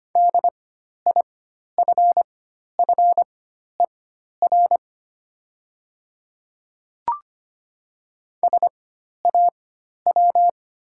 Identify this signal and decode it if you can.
Morse code, heard as a keyed tone, about 25 wpm, spaced as for 12 wpm, 700 Hz. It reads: DIFFER SAW
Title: 4x Character Spacing - Top 500 Words - 25wpm